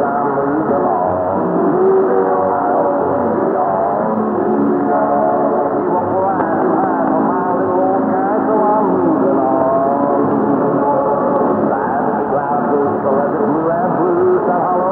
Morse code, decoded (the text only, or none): none